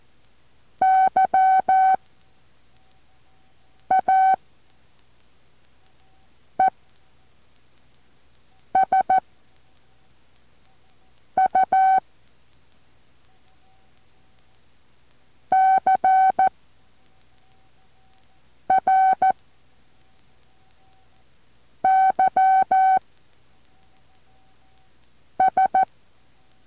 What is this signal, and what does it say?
Morse code, heard as a keyed tone, about 14 wpm, 760 Hz. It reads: Y A E S U C R Y S
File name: intl morse code2